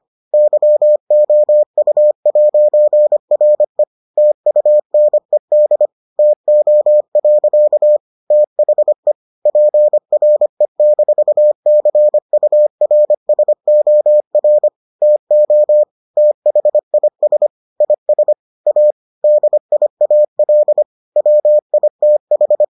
Morse code, heard as a keyed tone, about 25 wpm, 600 Hz. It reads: YOU'RE TUNED TO. THE PRE-CURSOR TO THIS IS A DIAL WITH